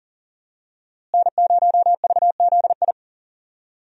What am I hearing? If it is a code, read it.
Morse code, heard as a keyed tone, about 40 wpm, 700 Hz. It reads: N0VZI
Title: USA Call Signs - 5 Characters - Repeated 2x - 40wpm